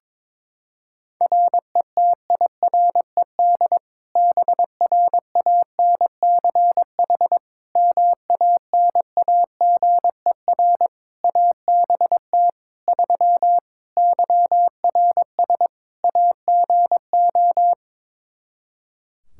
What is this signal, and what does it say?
Morse code, heard as a keyed tone, about 22 wpm, 700 Hz. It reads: RETIRED BRANCH MANAGER ABT 3 YRS AGO
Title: Practice QSO - Exchange 3 - More Info - 22wpm